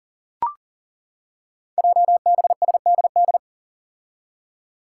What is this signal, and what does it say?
Morse code, heard as a keyed tone, about 38 wpm, 690 Hz. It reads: JBSDD
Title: Five Challenging Letter-Number Combinations 40wpm